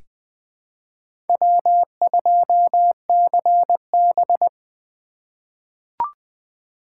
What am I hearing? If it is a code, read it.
Morse code, heard as a keyed tone, about 20 wpm, 700 Hz. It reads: W2CB
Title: USA Call Signs - 4 Characters - 3x Character Spacing - 20wpm